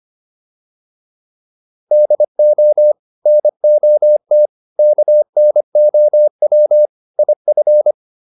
Morse code, heard as a keyed tone, about 25 wpm, 600 Hz. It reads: DO NOT KNOW IF
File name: WD304 25